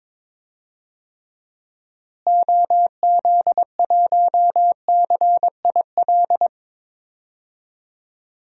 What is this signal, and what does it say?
Morse code, heard as a keyed tone, about 22 wpm, 700 Hz. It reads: OZ1CIL
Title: DXCC-100 Call Signs - 6 Characters - Repeated 2x - 22wpm